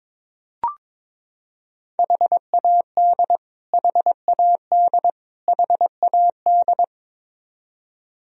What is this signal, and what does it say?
Morse code, heard as a keyed tone, about 22 wpm, 700 Hz. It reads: HAD HAD HAD
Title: Top 100 Words Repeated 22wpm